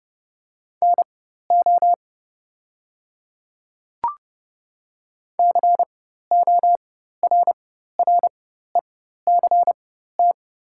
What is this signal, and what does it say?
Morse code, heard as a keyed tone, about 30 wpm, spaced as for 14 wpm, 690 Hz. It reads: NO CORRECT
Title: 4x Character Spacing - Top 500 Words - 30wpm